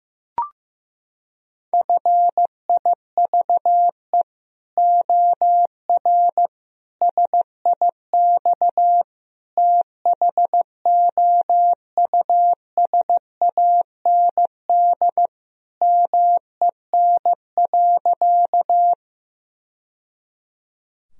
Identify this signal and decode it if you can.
Morse code, heard as a keyed tone, about 16 wpm, 700 Hz. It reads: FIVE OR SIX THOUSAND MEN.
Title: Sentences from Top 1000 Words - Review Part 3 - 15wpm